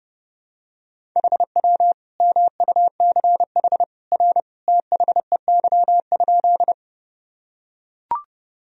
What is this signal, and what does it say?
Morse code, heard as a keyed tone, about 32 wpm, 700 Hz. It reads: HW MUCH R THEY?
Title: Sentences from Top 200 Words and Abbreviations 30wpm